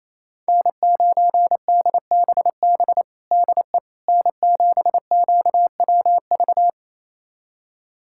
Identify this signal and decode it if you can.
Morse code, heard as a keyed tone, about 28 wpm, 700 Hz. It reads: N9DBB DE N7QWV
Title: USA Callsign DE Callsign - 5 Characters - 28wpm